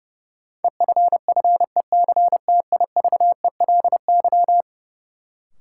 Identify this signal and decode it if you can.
Morse code, heard as a keyed tone, about 30 wpm, 700 Hz. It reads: EFFECTIVELY
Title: Sentences from Top 1400 Words - New Word Drill - 30wpm